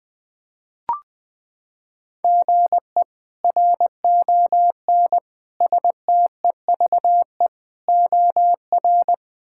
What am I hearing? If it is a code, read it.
Morse code, heard as a keyed tone, about 20 wpm, 700 Hz. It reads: GE RON STEVE OR
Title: K1USN SST QSO Practice - 20wpm